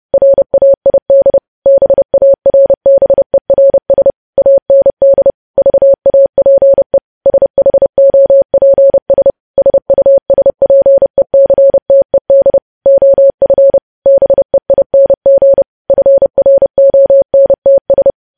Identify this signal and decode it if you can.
Morse code, heard as a keyed tone, about 30 wpm, 550 Hz. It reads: RAID BARBERS AND VAPE SHOPS SUSPECTED OF BEING FRONTS
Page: news headlines in morse code at 30 wpm